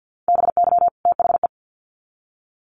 Morse code, heard as a keyed tone, about 50 wpm, 700 Hz. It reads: BY THE
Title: Sets of 2 Words 50wpm